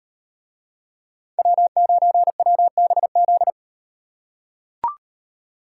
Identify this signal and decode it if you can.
Morse code, heard as a keyed tone, about 38 wpm, 700 Hz. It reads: W9WBZ